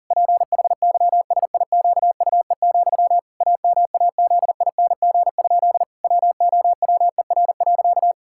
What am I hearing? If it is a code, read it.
Morse code, heard as a keyed tone, about 37 wpm, 700 Hz. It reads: PHYSIQUE, AMAZING? WOWER.
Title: Pangram Challenge v2 - 40wpm